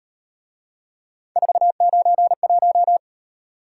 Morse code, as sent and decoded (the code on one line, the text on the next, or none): ....- ----. .----
491